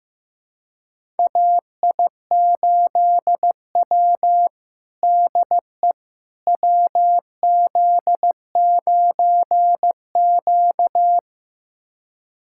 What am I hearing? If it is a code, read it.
Morse code, heard as a keyed tone, about 15 wpm, 700 Hz. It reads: AI8W DE WZ9Q
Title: USA Callsign DE Callsign - 4 Characters - 15wpm